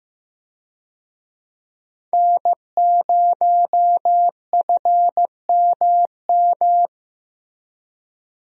Morse code, heard as a keyed tone, about 15 wpm, 700 Hz. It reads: N0FMM
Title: USA Call Signs - 5 Characters - Repeated 3x - 15wpm